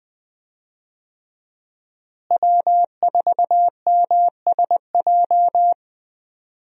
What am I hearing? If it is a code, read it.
Morse code, heard as a keyed tone, about 20 wpm, 700 Hz. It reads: W4MSJ
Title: USA Call Signs - 5 Characters - Repeated 2x - 20wpm